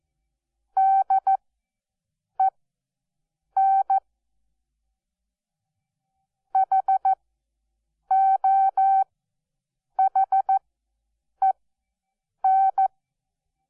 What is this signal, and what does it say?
Morse code, heard as a keyed tone, about 14 wpm, spaced as for 7 wpm, 770 Hz. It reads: DEN HOHEN
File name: Onlinelehrgang / morsen / DJ4UF Morsen12